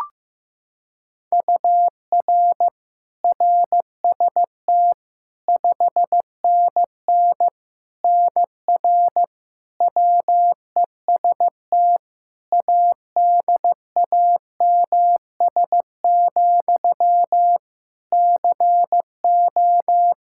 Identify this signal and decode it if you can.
Morse code, heard as a keyed tone, about 16 wpm, 700 Hz. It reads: UR RST 5NN NR WEST ADAMS, CO
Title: Practice QSO - Exchange 1 - Beginning - 15wpm